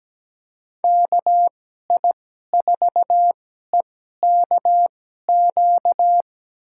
Morse code, heard as a keyed tone, about 17 wpm, spaced as for 12 wpm, 700 Hz. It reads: KI4EKQ
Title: USA Call Signs - 6 Characters - 2x Character Spacing - 17wpm